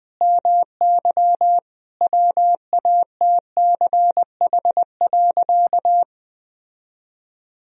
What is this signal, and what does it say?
Morse code, heard as a keyed tone, about 20 wpm, 700 Hz. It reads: MY WATCH.
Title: Sentences from Top 1000 Words - Review Part 1 - 20wpm